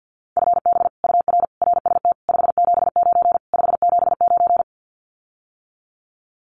Morse code, heard as a keyed tone, about 50 wpm, 700 Hz. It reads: FB UR RST 579 579
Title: Common QSO Phrases Encore 50wpm